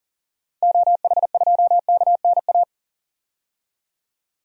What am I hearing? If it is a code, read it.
Morse code, heard as a keyed tone, about 40 wpm, 700 Hz. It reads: OH2KNA
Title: DXCC-100 Call Signs - 6 Characters - 40wpm